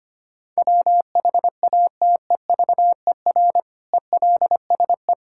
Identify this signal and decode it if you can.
Morse code, heard as a keyed tone, about 25 wpm, 700 Hz. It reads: WHATEVER ELSE
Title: Sentences from Top 1600 Words 25wpm